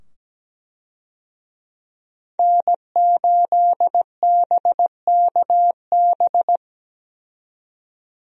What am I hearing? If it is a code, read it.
Morse code, heard as a keyed tone, about 17 wpm, 700 Hz. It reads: N8BKB
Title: USA Call Signs - 5 Characters - 17wpm